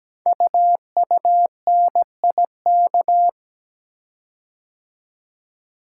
Morse code, heard as a keyed tone, about 17 wpm, 700 Hz. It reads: UUNIK
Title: Code Groups - 5 Letters - 17wpm